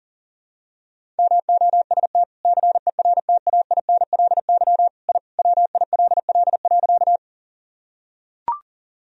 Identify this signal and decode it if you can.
Morse code, heard as a keyed tone, about 40 wpm, 700 Hz. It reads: MOST CERTAINLY I WILL.
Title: Sentences from Top 1000 Words - Review Part 1 - 40wpm